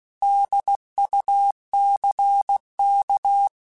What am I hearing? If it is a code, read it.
Morse code, heard as a keyed tone, about 16 wpm, 800 Hz. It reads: DUCK
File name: Alphabetical / Alphabetical mp3s / Ds @15